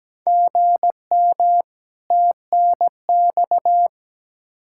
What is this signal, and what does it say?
Morse code, heard as a keyed tone, about 17 wpm, 700 Hz. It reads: GM TNX
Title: Practice QSO - Exchange 1 - Beginning - 17wpm